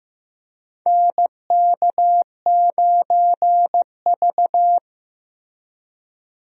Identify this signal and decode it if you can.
Morse code, heard as a keyed tone, about 15 wpm, 700 Hz. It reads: NK9V
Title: USA Call Signs - 4 Characters - 15wpm